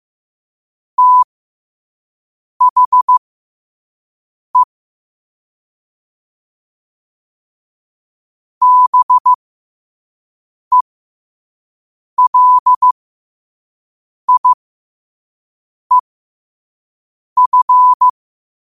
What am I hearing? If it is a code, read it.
Morse code, heard as a keyed tone, about 15 wpm, spaced as for 5 wpm, 1000 Hz. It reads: THE BELIEF
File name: qotdfiles / 5wpm